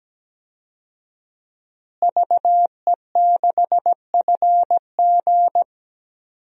Morse code, heard as a keyed tone, about 17 wpm, 700 Hz. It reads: VE6FG
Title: DXCC-100 Call Signs - 5 Characters - Repeated 3x - 17wpm